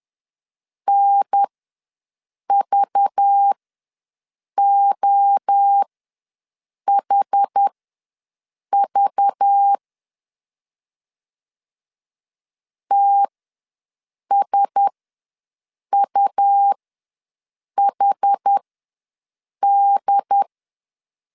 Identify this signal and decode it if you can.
Morse code, heard as a keyed tone, about 10 wpm, spaced as for 6 wpm, 800 Hz. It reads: NVOHV TSUHD
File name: LEZIONI / 5 LBVC / 5 LBV ws 10wpm 3645